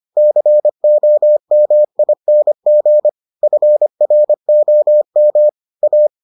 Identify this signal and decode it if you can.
Morse code, heard as a keyed tone, about 25 wpm, 600 Hz. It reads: COMING FROM A